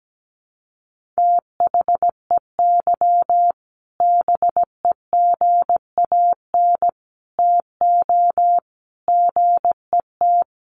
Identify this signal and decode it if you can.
Morse code, heard as a keyed tone, about 17 wpm, 700 Hz. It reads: THEY BEGAN TO GET
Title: Sentences from Top 1500 Words 17wpm